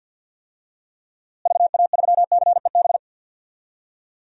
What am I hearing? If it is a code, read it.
Morse code, heard as a keyed tone, about 46 wpm, 700 Hz. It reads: VA3CEB
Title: DXCC-100 Call Signs - 6 Characters - Repeated 2x - 50wpm